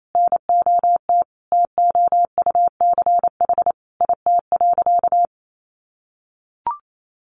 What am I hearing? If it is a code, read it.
Morse code, heard as a keyed tone, about 28 wpm, 700 Hz. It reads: NOT TOUCH IT.
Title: Sentences from Top 1200 Words 28wpm